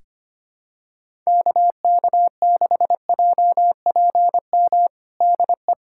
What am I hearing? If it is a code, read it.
Morse code, heard as a keyed tone, about 25 wpm, 700 Hz. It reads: KK6JPM DE
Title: USA Callsign DE Callsign - 6 Characters - 25wpm